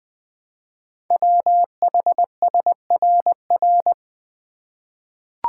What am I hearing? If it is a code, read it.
Morse code, heard as a keyed tone, about 20 wpm, 700 Hz. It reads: WHSRR
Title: Code Groups - 5 Letters - 20wpm